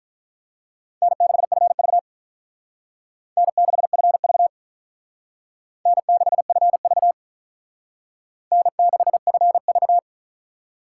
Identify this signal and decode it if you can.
Morse code, heard as a keyed tone, about 43 wpm, 700 Hz. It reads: N6FV N6FV N6FV N6FV
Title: Speed Racing - USA 1x2 Call Signs 35wpm